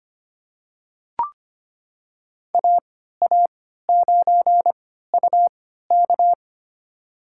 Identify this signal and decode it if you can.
Morse code, heard as a keyed tone, about 25 wpm, spaced as for 14 wpm, 700 Hz. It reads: AA9UK
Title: USA Call Signs - 3x Character Spacing - 25wpm